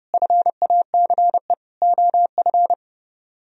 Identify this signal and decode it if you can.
Morse code, heard as a keyed tone, about 30 wpm, 700 Hz. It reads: FACE OF